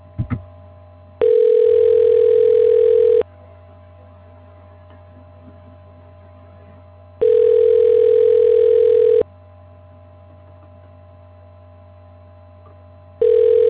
Ringback tone (North America), 440 + 480 Hz.